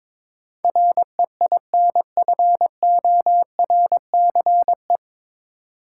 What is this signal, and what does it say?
Morse code, heard as a keyed tone, about 22 wpm, 700 Hz. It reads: REINFORCE
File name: 9 Letter Words - First 3 Letters Given - 22wpm